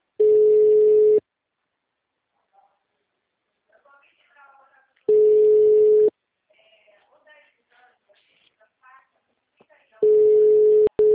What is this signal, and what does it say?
Ringback tone (Europe, 425 Hz), with speech.